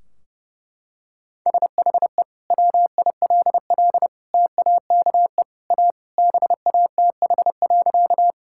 Morse code, heard as a keyed tone, about 30 wpm, 700 Hz. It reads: SHE WILL TAKE A BATH.
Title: Sentences from Top 1600 Words 30wpm